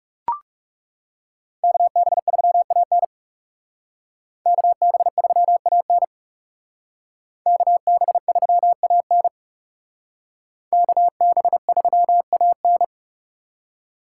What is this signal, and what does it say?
Morse code, heard as a keyed tone, about 37 wpm, 700 Hz. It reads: KB3AN KB3AN KB3AN KB3AN
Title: Speed Racing - USA 2x2 Call Signs 30wpm